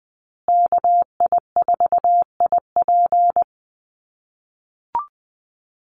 Morse code, heard as a keyed tone, about 20 wpm, 700 Hz. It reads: KI4IP
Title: USA Call Signs Repeated 3x 20wpm